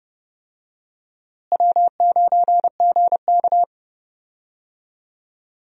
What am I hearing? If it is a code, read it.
Morse code, heard as a keyed tone, about 30 wpm, 700 Hz. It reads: W9GK
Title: USA Call Signs - 4 Characters - Repeated 3x - 30wpm